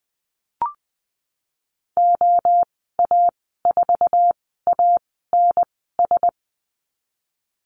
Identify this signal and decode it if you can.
Morse code, heard as a keyed tone, about 20 wpm, spaced as for 15 wpm, 700 Hz. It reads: OA4ANS
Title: 055 Lesson 10g - Letter r - Call Signs 20wpm